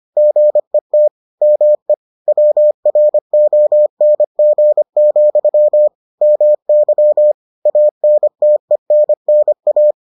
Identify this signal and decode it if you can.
Morse code, heard as a keyed tone, about 25 wpm, 600 Hz. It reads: GET ME WRONG, MY ANTENNA